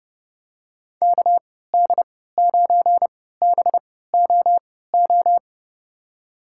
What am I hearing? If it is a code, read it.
Morse code, heard as a keyed tone, about 32 wpm, spaced as for 17 wpm, 700 Hz. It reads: KD9BOO
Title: USA Call Signs - 3x Character Spacing - 30wpm